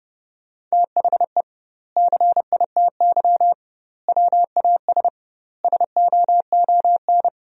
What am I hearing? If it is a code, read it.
Morse code, heard as a keyed tone, about 32 wpm, 700 Hz. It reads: THE CITY WAS SOON
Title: 2x Word-Spacing - Sentences from Top 500 Words - 30wpm